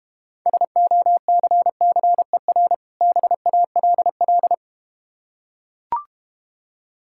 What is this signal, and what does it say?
Morse code, heard as a keyed tone, about 32 wpm, 700 Hz. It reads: SOCCER BALL